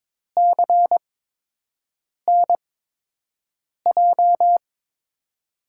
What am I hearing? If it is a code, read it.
Morse code, heard as a keyed tone, about 22 wpm, spaced as for 6 wpm, 700 Hz. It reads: CNJ
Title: USA Call Signs - 6 Characters - 8x Character Spacing - 22wpm